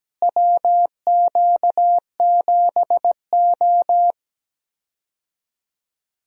Morse code, heard as a keyed tone, about 17 wpm, 700 Hz. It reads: WQ7O
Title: USA Call Signs - 4 Characters - Repeated 2x - 17wpm